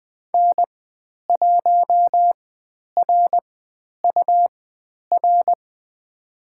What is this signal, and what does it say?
Morse code, heard as a keyed tone, about 20 wpm, spaced as for 10 wpm, 700 Hz. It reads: N1RUR
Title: 102 Lesson 17g - Letter u - Call Signs 20wpm